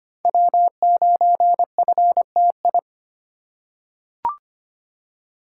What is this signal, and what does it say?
Morse code, heard as a keyed tone, about 25 wpm, 700 Hz. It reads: W9FTI
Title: USA Call Signs - 3x Character Spacing - 25wpm